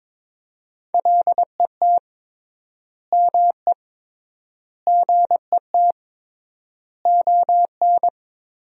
Morse code, heard as a keyed tone, about 22 wpm, 700 Hz. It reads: LET ME GET ON
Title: 3x Word-Spacing - Sentences from Top 300 Words - 22wpm